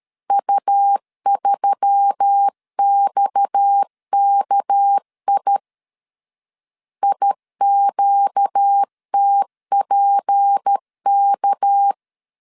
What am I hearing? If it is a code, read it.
Morse code, heard as a keyed tone, about 13 wpm, 800 Hz. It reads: U3XKI IQTPK
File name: CW / DET / 1 6 1500C S A 12wpm 3440